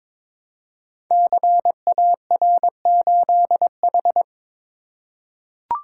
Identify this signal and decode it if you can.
Morse code, heard as a keyed tone, about 22 wpm, 700 Hz. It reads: CAR8H
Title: Code Groups - 5 Letter:Numbers - 22wpm